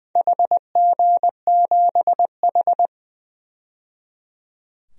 Morse code, heard as a keyed tone, about 20 wpm, 700 Hz. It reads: HG7H
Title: DXCC-100 Call Signs - 4 Characters - 20wpm